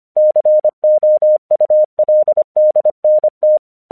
Morse code, heard as a keyed tone, about 25 wpm, 610 Hz. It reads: COULDNT
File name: 1 English 25wpm / AQC30